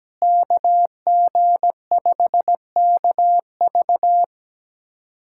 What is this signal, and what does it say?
Morse code, heard as a keyed tone, about 17 wpm, 700 Hz. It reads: KG5KV
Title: USA Callsign DE Callsign - 5 Characters - 17wpm